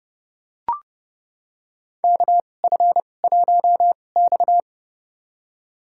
Morse code, heard as a keyed tone, about 31 wpm, spaced as for 22 wpm, 700 Hz. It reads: KF1X
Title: USA Call Signs - 2x Character Spacing - 30wpm